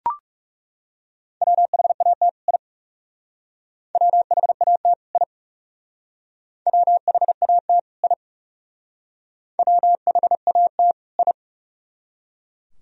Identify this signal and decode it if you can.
Morse code, heard as a keyed tone, about 37 wpm, 700 Hz. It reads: WHAT I WHAT I WHAT I WHAT I
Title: Speed-Racing - Sets of 2 Words 30wpm